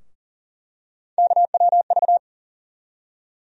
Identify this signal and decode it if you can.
Morse code, heard as a keyed tone, about 37 wpm, 700 Hz. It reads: KWV
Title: USA Call Sign Postfixes - 3 Letters 40wpm